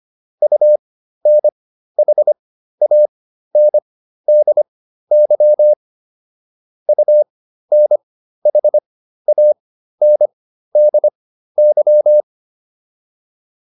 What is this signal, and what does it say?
Morse code, heard as a keyed tone, about 25 wpm, spaced as for 13 wpm, 600 Hz. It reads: UNHANDY UNHANDY
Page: intermediate-practice-files